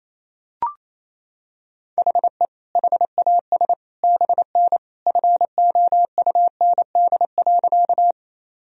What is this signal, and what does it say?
Morse code, heard as a keyed tone, about 28 wpm, 700 Hz. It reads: HE HAS BN FOUND.
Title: Sentences from Top 300 Words and Abbreviations 28wpm